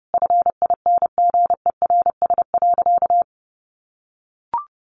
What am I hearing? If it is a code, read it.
Morse code, heard as a keyed tone, about 30 wpm, 700 Hz. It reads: FINGERS.
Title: A Sound Of Thunder 30wpm